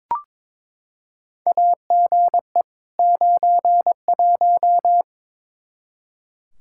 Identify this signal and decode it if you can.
Morse code, heard as a keyed tone, about 22 wpm, 700 Hz. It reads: AGE 91
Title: Practice QSO - Exchange 3 - More Info - 22wpm